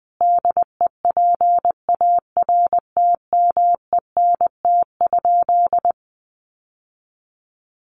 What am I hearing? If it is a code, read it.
Morse code, heard as a keyed tone, about 20 wpm, 700 Hz. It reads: DEPARTMENT?
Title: Sentences from Top 1200 Words 20wpm